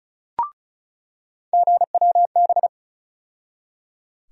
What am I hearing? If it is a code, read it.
Morse code, heard as a keyed tone, about 36 wpm, 700 Hz. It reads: GWB
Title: USA Call Sign Postfixes - 3 Letters 35wpm